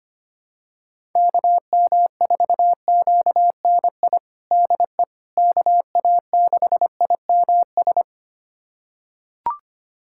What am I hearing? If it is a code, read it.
Morse code, heard as a keyed tone, about 25 wpm, 700 Hz. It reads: KM4QNI DE KA6IMS
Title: USA Callsign DE Callsign - 6 Characters - 25wpm